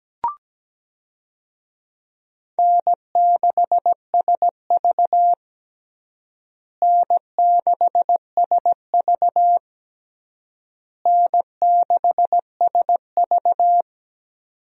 Morse code, heard as a keyed tone, about 17 wpm, 700 Hz. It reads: N6SV N6SV N6SV
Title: USA Call Signs - 4 Characters - Repeated 3x - 17wpm